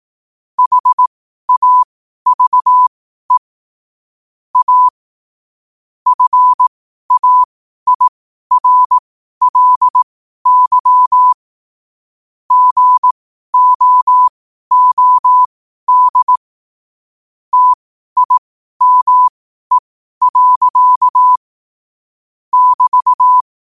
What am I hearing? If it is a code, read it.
Morse code, heard as a keyed tone, about 18 wpm, spaced as for 12 wpm, 1000 Hz. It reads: HAVE A FAIRLY GOOD TIME. <BT>
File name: qotdfiles / 12wpm